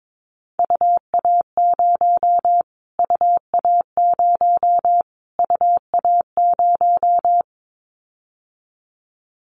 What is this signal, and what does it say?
Morse code, heard as a keyed tone, about 22 wpm, 700 Hz. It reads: UA0 UA0 UA0
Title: Callsign Prefixes for Top 25 DXCC - Repeated 3x - 22wpm